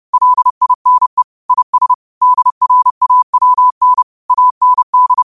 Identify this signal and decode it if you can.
Morse code, heard as a keyed tone, about 30 wpm, 1000 Hz. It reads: LINE IS DRAWN AND
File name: qotdfiles / 30wpm